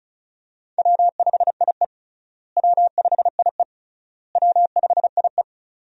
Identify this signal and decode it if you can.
Morse code, heard as a keyed tone, about 36 wpm, 700 Hz. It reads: W5IE W5IE W5IE
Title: USA Call Signs - 4 Characters - Repeated 3x - 35wpm